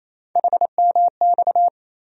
Morse code, heard as a keyed tone, about 28 wpm, 700 Hz. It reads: HMX
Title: USA Call Sign Postfixes - 3 Letters 28wpm